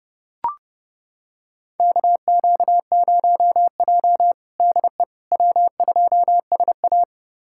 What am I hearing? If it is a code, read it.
Morse code, heard as a keyed tone, about 32 wpm, 700 Hz. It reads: KQ0J DE W2SA
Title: USA Callsign DE Callsign - 4 Characters - 30wpm